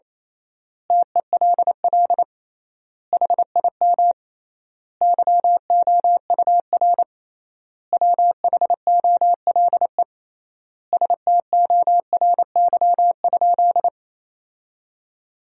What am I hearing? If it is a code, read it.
Morse code, heard as a keyed tone, about 28 wpm, 700 Hz. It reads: TELL HIM YOUR WHOLE STORY?
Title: 3x Word-Spacing - Sentences from Top 400 Words - 28wpm